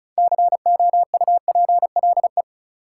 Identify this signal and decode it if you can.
Morse code, heard as a keyed tone, about 36 wpm, 700 Hz. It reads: COUPLE
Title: Sets of 2 Words - New Word Drill - 35wpm